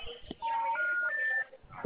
Telephone call progress tone, heard as a telephone network's tone, with noise. Special information tone.